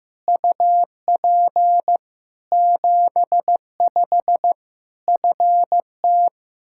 Morse code, heard as a keyed tone, about 15 wpm, 700 Hz. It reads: UP 75 FT
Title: K1USN SST QSO Practice 15wpm